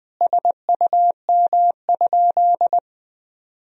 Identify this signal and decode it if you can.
Morse code, heard as a keyed tone, about 21 wpm, 700 Hz. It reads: SUM?
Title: Sentences from Top 100 Words and Abbreviations 20wpm